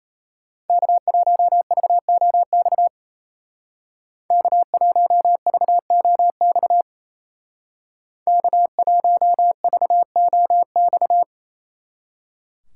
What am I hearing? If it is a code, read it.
Morse code, heard as a keyed tone, about 34 wpm, 700 Hz. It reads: K1VOX K1VOX K1VOX
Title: Speed Racing - USA 1x3 Call Signs 28wpm